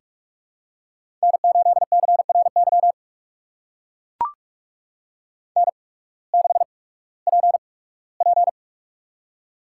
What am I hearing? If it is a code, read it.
Morse code, heard as a keyed tone, about 44 wpm, spaced as for 21 wpm, 700 Hz. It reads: N8CRY N6PP
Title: USA Call Signs - 8x Character Spacing - 45wpm